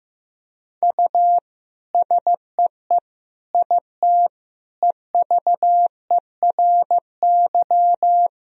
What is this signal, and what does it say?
Morse code, heard as a keyed tone, about 15 wpm, 700 Hz. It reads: U SEE IT EVERY
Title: Sentences from Top 200 Words and Abbreviations 15wpm